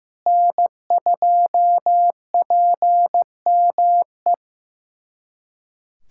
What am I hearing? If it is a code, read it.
Morse code, heard as a keyed tone, about 15 wpm, 700 Hz. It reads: N2PME
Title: USA Callsign DE Callsign - 5 Characters - 15wpm